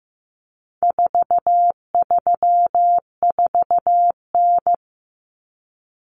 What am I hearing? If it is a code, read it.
Morse code, heard as a keyed tone, about 15 wpm, 700 Hz. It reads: 434N
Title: Common Cut Numbers - 4-Digit 15wpm